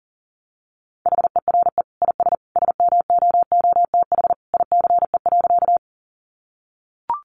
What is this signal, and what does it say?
Morse code, heard as a keyed tone, about 40 wpm, 700 Hz. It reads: HERE IS SMOOTH ICE.
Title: Sentences from Top 1400 Words 40wpm